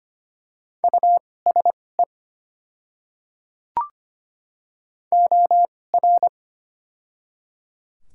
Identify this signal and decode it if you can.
Morse code, heard as a keyed tone, about 25 wpm, spaced as for 18 wpm, 700 Hz. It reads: USE OR
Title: 2x Character Spacing - Top 100 Words - 25wpm